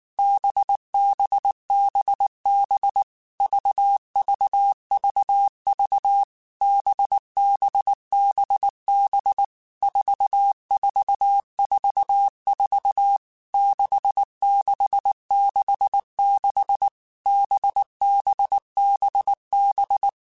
Morse code, heard as a keyed tone, about 19 wpm, 800 Hz. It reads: BBBB VVVV BBBB 4444 6666 BBBB
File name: Basic Letter Groups By7Groups / VB46 / VB46@18